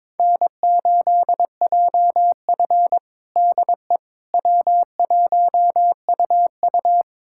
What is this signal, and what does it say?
Morse code, heard as a keyed tone, about 22 wpm, 700 Hz. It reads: N8JF DE W1UU
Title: USA Callsign DE Callsign - 4 Characters - 22wpm